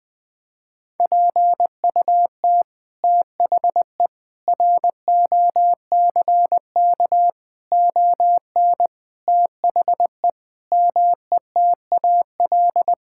Morse code, heard as a keyed tone, about 20 wpm, 700 Hz. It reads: PUT THE ROCK ON THE METAL
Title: Sentences from Top 1000 Words - Review Part 6 - 20wpm